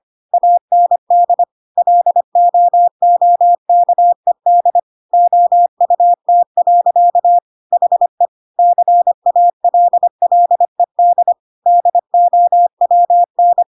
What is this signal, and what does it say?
Morse code, heard as a keyed tone, about 25 wpm, 700 Hz. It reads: AND LOOKED OUT. HE CALLED DOWN